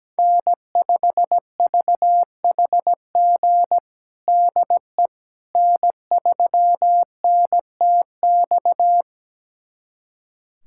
Morse code, heard as a keyed tone, about 17 wpm, 700 Hz. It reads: N5VHG DE N3NTX
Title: USA Callsign DE Callsign - 5 Characters - 17wpm